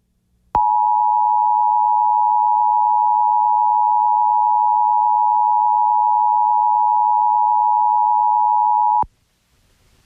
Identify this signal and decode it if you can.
Emergency Alert System (SAME) tones (the 853 and 960 Hz attention tone). It is heard from 0.6 to 9.0 s.